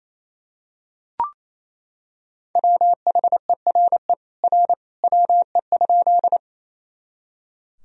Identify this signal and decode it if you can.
Morse code, heard as a keyed tone, about 28 wpm, 700 Hz. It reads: WHERE R WE?
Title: Sentences from Top 200 Words and Abbreviations 28wpm